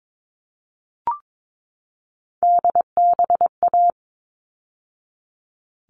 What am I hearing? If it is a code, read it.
Morse code, heard as a keyed tone, about 22 wpm, 700 Hz. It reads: DBA
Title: All Cut Numbers - 3-Digit 22wpm